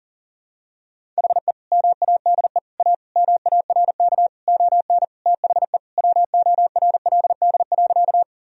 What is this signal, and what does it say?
Morse code, heard as a keyed tone, about 40 wpm, 700 Hz. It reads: HE MADE A MARK ON THE WORLD.
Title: Sentences from Top 1000 Words - Review Part 1 - 40wpm